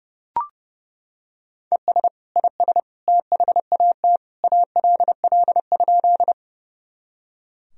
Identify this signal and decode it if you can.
Morse code, heard as a keyed tone, about 30 wpm, 700 Hz. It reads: ES IS THAT ALL?
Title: Sentences from Top 100 Words and Abbreviations 30wpm